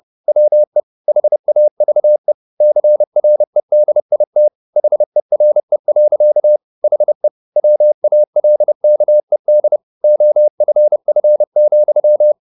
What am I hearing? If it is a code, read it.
Morse code, heard as a keyed tone, about 30 wpm, 600 Hz. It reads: WE HAVE CREDIT HERE. HE WALKED OFF,
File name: the old man and the sea / the old man and the sea 1 30wpm